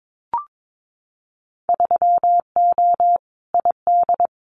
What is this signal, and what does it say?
Morse code, heard as a keyed tone, about 22 wpm, 700 Hz. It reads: 3O ID
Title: Winter Field Day Exchanges Encore 22wpm